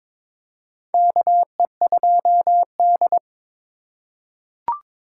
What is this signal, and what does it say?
Morse code, heard as a keyed tone, about 22 wpm, 700 Hz. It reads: KE2D
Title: K1USN SST QSO Practice - 22wpm